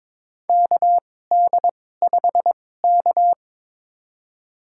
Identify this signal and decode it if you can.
Morse code, heard as a keyed tone, about 22 wpm, spaced as for 16 wpm, 700 Hz. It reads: KD5K
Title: USA Call Signs - 2x Character Spacing - 22wpm